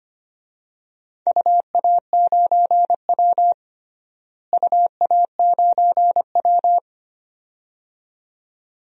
Morse code, heard as a keyed tone, about 25 wpm, 700 Hz. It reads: UA9W UA9W
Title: DXCC-100 Call Signs - 4 Characters - Repeated 2x - 25wpm